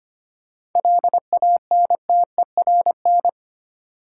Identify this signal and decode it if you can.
Morse code, heard as a keyed tone, about 25 wpm, 690 Hz. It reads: LANTERN
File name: practice / 25wpm / Easy Words with Hints - 25wpm